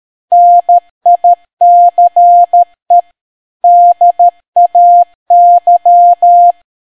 Morse code, heard as a keyed tone, about 13 wpm, 700 Hz. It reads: NICE DAY